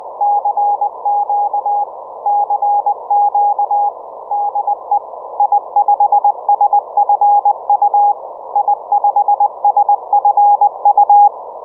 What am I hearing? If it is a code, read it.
Morse code, heard as a keyed tone, about 20 wpm, 820 Hz. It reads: CQ CQ DE I5SFU I5SFU